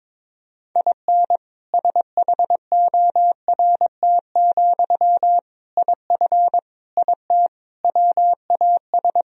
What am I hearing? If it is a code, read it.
Morse code, heard as a keyed tone, about 22 wpm, 700 Hz. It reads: IN SHORT, IF IT WAS
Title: Sentences from Top 1000 Words - Review Part 2 - 22wpm